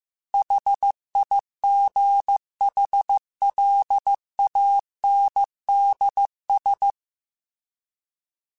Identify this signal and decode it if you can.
Morse code, heard as a keyed tone, about 15 wpm, 800 Hz. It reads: HIGHLANDS
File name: fromSMs / FromSMs mp3s / Cities@13